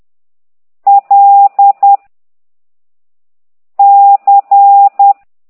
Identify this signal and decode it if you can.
Morse code, heard as a keyed tone, about 10 wpm, 800 Hz. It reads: L C
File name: ct2idw / exerc9